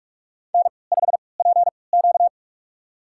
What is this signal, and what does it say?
Morse code, heard as a keyed tone, about 46 wpm, spaced as for 26 wpm, 700 Hz. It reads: N5PQ
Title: USA Call Signs - 3x Character Spacing - 45wpm